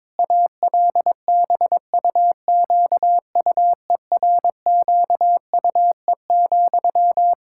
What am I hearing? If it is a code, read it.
Morse code, heard as a keyed tone, about 22 wpm, 700 Hz. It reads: ALBUQUERQUE,
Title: Practice QSO - Exchange 1 - Beginning - 22wpm